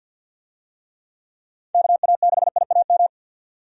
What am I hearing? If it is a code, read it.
Morse code, heard as a keyed tone, about 51 wpm, 700 Hz. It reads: KA6IAM
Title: USA Call Signs - 6 Characters - 50wpm